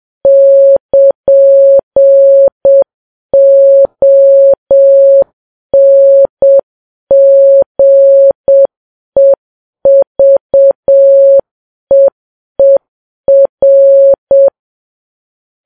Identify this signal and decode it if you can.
Morse code, heard as a keyed tone, about 7 wpm, 550 Hz. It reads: <KN>ONGEVEER